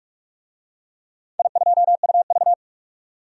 Morse code, heard as a keyed tone, about 44 wpm, 700 Hz. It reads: I2UV
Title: DXCC-100 Call Signs - 4 Characters - 45wpm